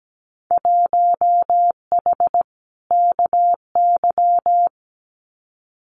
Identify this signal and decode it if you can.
Morse code, heard as a keyed tone, about 17 wpm, 700 Hz. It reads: 1H KY
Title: Winter Field Day Exchanges 17wpm